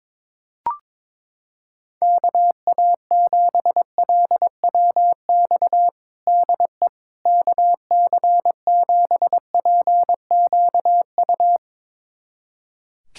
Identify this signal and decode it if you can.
Morse code, heard as a keyed tone, about 22 wpm, 700 Hz. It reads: KA7LWX DE KC7PQU
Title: USA Callsign DE Callsign - 6 Characters - 22wpm